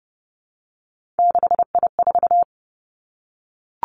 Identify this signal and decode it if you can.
Morse code, heard as a keyed tone, about 30 wpm, 700 Hz. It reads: 6I4
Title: Three Challenging Letter-Number Combinations 30wpm